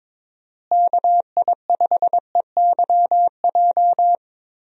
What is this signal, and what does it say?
Morse code, heard as a keyed tone, about 22 wpm, 700 Hz. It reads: KI5EYJ
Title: USA Call Signs - 6 Characters - Repeated 2x - 22wpm